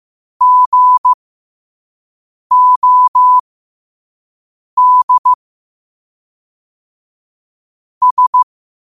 Morse code, heard as a keyed tone, about 15 wpm, spaced as for 5 wpm, 1000 Hz. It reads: GOD S
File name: qotdfiles / 5wpm